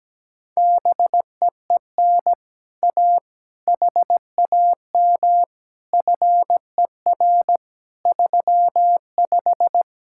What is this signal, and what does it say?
Morse code, heard as a keyed tone, about 17 wpm, 700 Hz. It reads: BEEN A HAM FER 35
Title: Speed-Racing Common QSO Phrases Encore - 17wpm